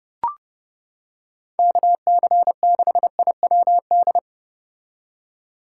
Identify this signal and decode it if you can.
Morse code, heard as a keyed tone, about 29 wpm, 700 Hz. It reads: KC6IWD